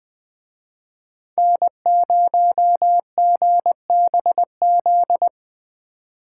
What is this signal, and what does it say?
Morse code, heard as a keyed tone, about 20 wpm, 700 Hz. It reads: N0GBZ
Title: Speed Racing - USA 1x3 Call Signs 17wpm